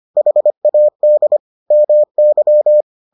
Morse code, heard as a keyed tone, about 25 wpm, 600 Hz. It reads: HAD MY